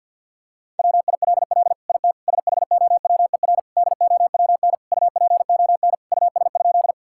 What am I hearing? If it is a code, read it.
Morse code, heard as a keyed tone, about 50 wpm, 700 Hz. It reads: WILL IT SHOWER DOWN UPON US?
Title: Sentences from Top 1600 Words 50wpm